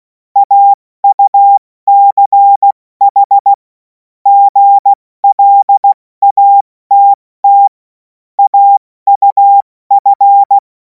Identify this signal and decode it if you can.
Morse code, heard as a keyed tone, about 16 wpm, spaced as for 14 wpm, 800 Hz. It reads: AUCH GLATT AUF
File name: Morsekurs / Witze / Witz 11. August 2026 70 F80